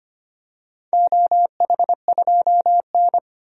Morse code, heard as a keyed tone, about 25 wpm, 700 Hz. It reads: OH2N
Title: DXCC-100 Call Signs - 4 Characters - Repeated 2x - 25wpm